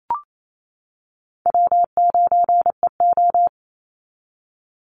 Morse code, heard as a keyed tone, about 28 wpm, 700 Hz. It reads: W9EO
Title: Easy Call Signs - 28wpm